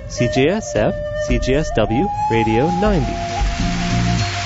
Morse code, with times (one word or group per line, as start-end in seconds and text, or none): none